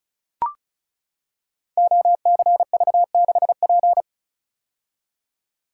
Morse code, heard as a keyed tone, about 36 wpm, 700 Hz. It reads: OCV6P
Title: Code Groups - 5 Letter:Numbers - 35wpm